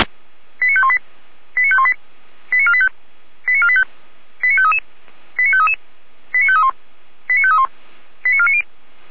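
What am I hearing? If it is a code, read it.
Five-tone selective call (ZVEI), heard as quick runs of five tones, tones of about 70 ms. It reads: 89518 89518 89485 89485 89533 89533 89531 89531 89490
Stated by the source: ZVEI five-tone call received by SDR, from Feuerwehr Orlamünde.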